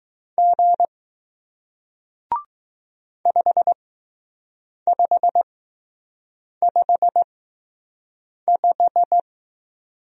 Morse code, heard as a keyed tone, about 20 wpm, spaced as for 6 wpm, 700 Hz. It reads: G 5555